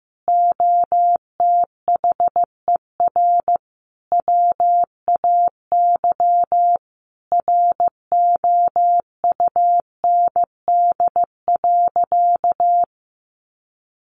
Morse code, heard as a keyed tone, about 14 wpm, 700 Hz. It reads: OTHER WAY ROUND.
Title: Sentences from Top 200 Words 15wpm